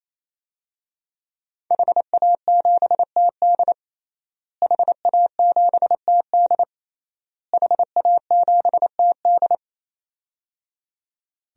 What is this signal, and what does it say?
Morse code, heard as a keyed tone, about 28 wpm, 700 Hz. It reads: HA7TD HA7TD HA7TD
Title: DXCC-100 Call Signs - 5 Characters - Repeated 3x - 28wpm